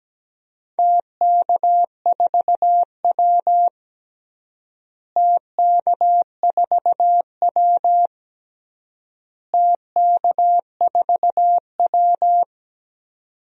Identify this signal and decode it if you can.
Morse code, heard as a keyed tone, about 17 wpm, 700 Hz. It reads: TK4W TK4W TK4W
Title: DXCC-100 Call Signs - 4 Characters - Repeated 3x - 17wpm